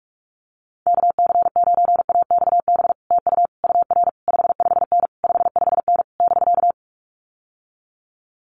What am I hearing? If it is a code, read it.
Morse code, heard as a keyed tone, about 46 wpm, 700 Hz. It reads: KC9AXB TU UR 55N 55N <BK>
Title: Common QSO Phrases Encore 45wpm